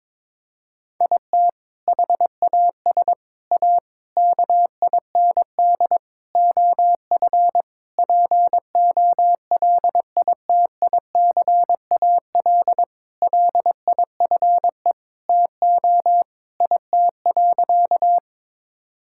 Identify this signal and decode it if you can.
Morse code, heard as a keyed tone, about 22 wpm, 700 Hz. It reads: IT HAS A KIND OF POLITICAL LIFE TO IT.
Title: Sentences from Top 1000 Words - Review Part 5 - 22wpm